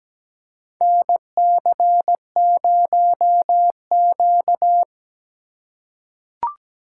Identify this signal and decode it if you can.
Morse code, heard as a keyed tone, about 17 wpm, 700 Hz. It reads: NC0Q
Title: USA Call Signs - 4x Character Spacing - 17wpm